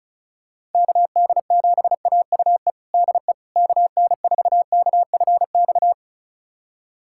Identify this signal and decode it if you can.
Morse code, heard as a keyed tone, about 34 wpm, 690 Hz. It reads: KD7AUE DE KN4KFX
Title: USA Callsign DE Callsign - 6 Characters - 35wpm